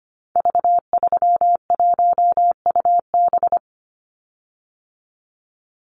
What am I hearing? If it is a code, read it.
Morse code, heard as a keyed tone, about 25 wpm, 700 Hz. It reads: V31UB
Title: Call Signs Repeated 2x - 25wpm